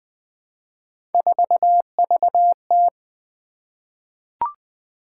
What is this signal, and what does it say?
Morse code, heard as a keyed tone, about 20 wpm, 700 Hz. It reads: 4VT
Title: Three Letter-Number Combinations - 20wpm